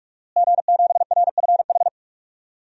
Morse code, heard as a keyed tone, about 46 wpm, 700 Hz. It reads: G7RFH
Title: DXCC-100 Call Signs - 5 Characters - Repeated 2x - 45wpm